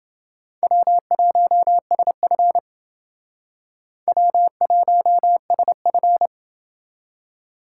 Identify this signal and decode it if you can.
Morse code, heard as a keyed tone, about 28 wpm, 700 Hz. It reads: W1SF W1SF